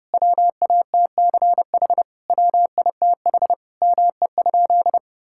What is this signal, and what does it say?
Morse code, heard as a keyed tone, about 30 wpm, 700 Hz. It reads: WATCH WITH ME?
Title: Sentences from Top 1000 Words - Review Part 5 - 30wpm